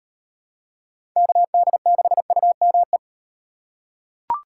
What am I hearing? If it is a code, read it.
Morse code, heard as a keyed tone, about 38 wpm, 700 Hz. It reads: KD6UME